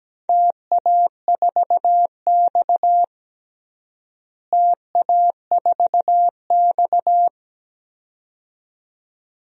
Morse code, heard as a keyed tone, about 17 wpm, 700 Hz. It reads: TA4X TA4X
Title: DXCC-100 Call Signs - 4 Characters - Repeated 2x - 17wpm